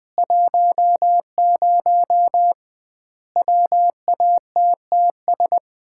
Morse code, heard as a keyed tone, about 20 wpm, 700 Hz. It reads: 10 WATTS
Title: Common QSO Phrases - 2x Word-Spacing - 20wpm